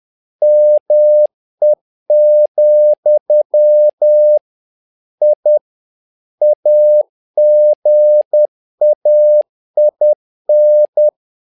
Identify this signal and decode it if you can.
Morse code, heard as a keyed tone, about 10 wpm, 600 Hz. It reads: ME, I AGAIN
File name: learning center / tower trouble / tower trouble 10wpm